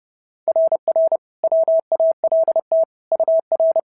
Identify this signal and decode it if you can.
Morse code, heard as a keyed tone, about 30 wpm, 690 Hz. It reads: RR WALT UR
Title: SOTA QSO Practice 30wpm